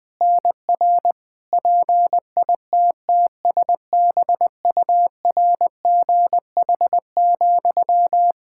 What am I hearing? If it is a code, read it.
Morse code, heard as a keyed tone, about 20 wpm, 700 Hz. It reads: NR PITTSBURGH,
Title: Practice QSO - Exchange 1 - Beginning - 20wpm